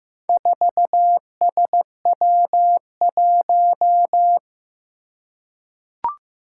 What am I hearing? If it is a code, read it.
Morse code, heard as a keyed tone, about 16 wpm, 700 Hz. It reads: 4SW1
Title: Four Challenging Letter-Number Combinations 15wpm